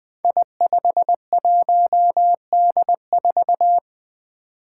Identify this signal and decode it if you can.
Morse code, heard as a keyed tone, about 20 wpm, 700 Hz. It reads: I51D4
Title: Code Groups - 5 Letter:Numbers - 20wpm